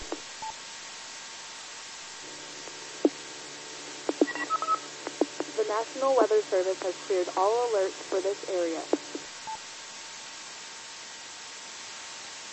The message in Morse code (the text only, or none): none